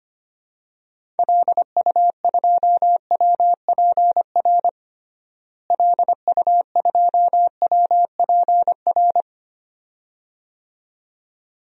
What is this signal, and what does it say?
Morse code, heard as a keyed tone, about 25 wpm, 700 Hz. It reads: LU2WPR LU2WPR
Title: DXCC-100 Call Signs - 6 Characters - Repeated 2x - 25wpm